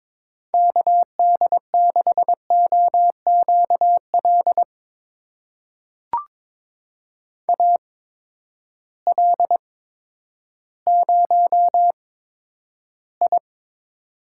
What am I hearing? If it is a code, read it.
Morse code, heard as a keyed tone, about 22 wpm, spaced as for 11 wpm, 700 Hz. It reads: KD6OQL AL0I
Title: USA Call Signs - 8x Character Spacing - 22wpm